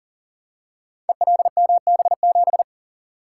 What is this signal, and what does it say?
Morse code, heard as a keyed tone, about 40 wpm, 700 Hz. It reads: ELMB7
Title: Code Groups - 5 Letter:Numbers - 40wpm